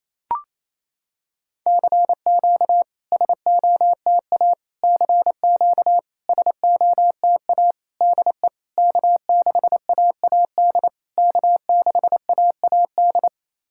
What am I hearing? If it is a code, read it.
Morse code, heard as a keyed tone, about 27 wpm, 690 Hz. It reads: CQ SOTA CQ SOTA DE K6AAD K6AAD
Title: SOTA QSO Practice 28wpm